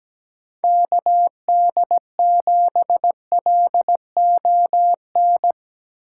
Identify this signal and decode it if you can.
Morse code, heard as a keyed tone, about 17 wpm, 700 Hz. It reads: KD7LON
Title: USA Call Signs - 6 Characters - 4x Character Spacing - 17wpm